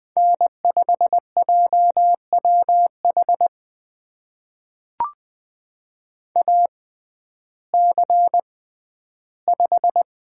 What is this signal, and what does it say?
Morse code, heard as a keyed tone, about 20 wpm, spaced as for 12 wpm, 700 Hz. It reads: N5JWH AC5
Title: USA Call Signs - 5 Characters - 6x Character Spacing - 20wpm